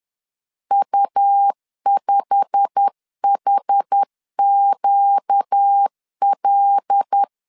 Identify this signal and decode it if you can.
Morse code, heard as a keyed tone, about 10 wpm, 800 Hz. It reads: U5HQL
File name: CW / DET / 1 9 1500C S A 10wpm 4511